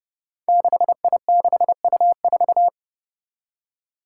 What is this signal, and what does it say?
Morse code, heard as a keyed tone, about 32 wpm, 700 Hz. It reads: 6I6U4
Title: Five Challenging Letter-Number Combinations 30wpm